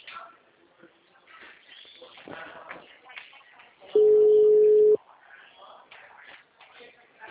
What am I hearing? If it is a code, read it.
Ringback tone (Europe, 425 Hz), with speech.
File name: out- -2002- - - .36